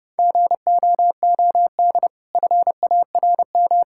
Morse code, heard as a keyed tone, about 30 wpm, 700 Hz. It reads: GOOD FARM
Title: Sentences from Top 1000 Words - Review Part 5 - 30wpm